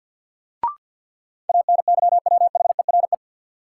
Morse code, heard as a keyed tone, about 51 wpm, 700 Hz. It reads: ANYWHERE
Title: Sentences from Top 1400 Words - New Word Drill Alt - 50wpm